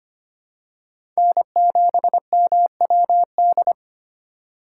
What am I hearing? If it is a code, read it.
Morse code, heard as a keyed tone, about 25 wpm, 700 Hz. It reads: N7MWD
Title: USA Call Signs - 5 Characters - 4x Character Spacing - 25wpm